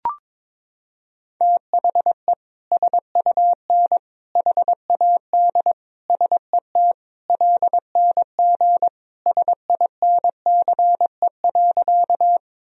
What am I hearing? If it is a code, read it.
Morse code, heard as a keyed tone, about 21 wpm, 700 Hz. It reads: THE SUN HAD SET LNG SINCE.
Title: Sentences from Top 300 Words and Abbreviations 22wpm